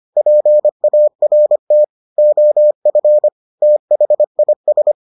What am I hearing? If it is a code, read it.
Morse code, heard as a keyed tone, about 25 wpm, 600 Hz. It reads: PART OF THIS